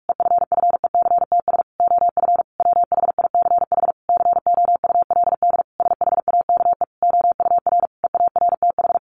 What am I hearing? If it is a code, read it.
Morse code, heard as a keyed tone, about 44 wpm, 700 Hz. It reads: EFFECTS OF WHICH COULD SHAKE OUR EARTH
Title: A Sound Of Thunder 45wpm